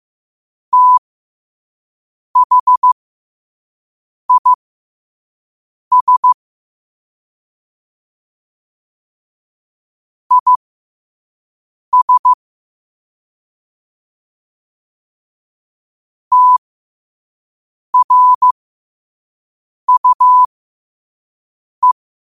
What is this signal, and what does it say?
Morse code, heard as a keyed tone, about 15 wpm, spaced as for 5 wpm, 1000 Hz. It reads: THIS IS TRUE
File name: qotdfiles / 5wpm